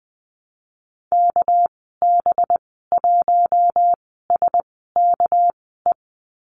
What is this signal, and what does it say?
Morse code, heard as a keyed tone, about 19 wpm, 700 Hz. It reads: K B 1 S K E
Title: 207 Lesson 33g - Letter j - Call Signs 20wpm